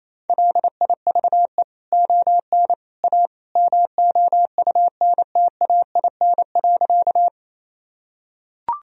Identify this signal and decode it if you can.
Morse code, heard as a keyed tone, about 28 wpm, 700 Hz. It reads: LIVE ON A MOUNTAIN.
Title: Sentences from Top 1000 Words - Review Part 1 - 28wpm